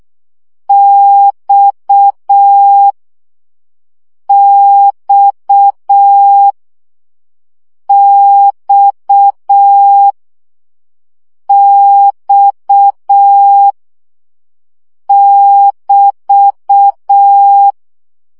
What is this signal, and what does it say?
Morse code, heard as a keyed tone, about 6 wpm, 800 Hz. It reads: X X X X <BT>
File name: CW gyakorlas x 5 baud